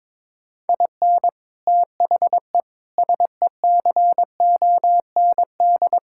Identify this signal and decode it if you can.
Morse code, heard as a keyed tone, about 22 wpm, 700 Hz. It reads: IN THE SECOND
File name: practice / 22wpm / Build A Phrase - 5-Word Phrases - 22wpm